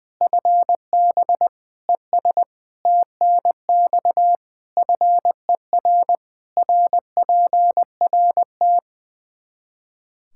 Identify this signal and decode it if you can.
Morse code, heard as a keyed tone, about 20 wpm, 700 Hz. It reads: FB ES TNX FER RPRT
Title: Practice QSO - Exchange 2 - Info - 20wpm